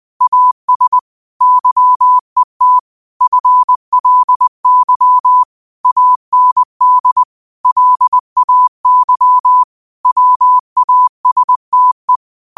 Morse code, heard as a keyed tone, about 20 wpm, 1000 Hz. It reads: AS YET FLY AND LAY WASTE
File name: qotdfiles / 20wpm